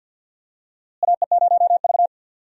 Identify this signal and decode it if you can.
Morse code, heard as a keyed tone, about 51 wpm, 700 Hz. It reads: AE0V
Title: USA Call Signs - 4 Characters - Repeated 3x - 50wpm